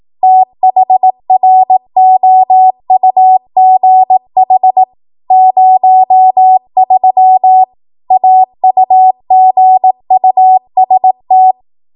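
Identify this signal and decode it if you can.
Morse code, heard as a keyed tone, about 18 wpm, 750 Hz. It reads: THROUGH 03 AUGUST
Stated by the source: ARRL 18 wpm Morse code practice archive transmission.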